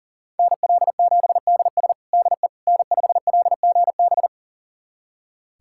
Morse code, heard as a keyed tone, about 40 wpm, 700 Hz. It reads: NL7DS DE N5LGB
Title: USA Callsign DE Callsign - 5 Characters - 40wpm